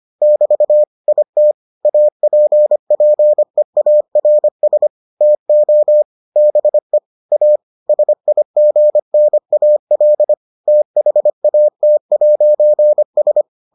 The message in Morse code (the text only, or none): <BT> IT APPEARS TO BE A SIGNAL THAT'S